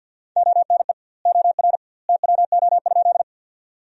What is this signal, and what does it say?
Morse code, heard as a keyed tone, about 46 wpm, 700 Hz. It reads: ONE OR TWO?
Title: 2x Word-Spacing - Sentences from Top 100 Words - 50wpm